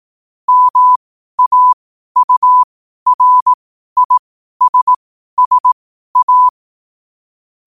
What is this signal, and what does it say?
Morse code, heard as a keyed tone, about 18 wpm, spaced as for 12 wpm, 1000 Hz. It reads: MAURISSA
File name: qotdfiles / 12wpm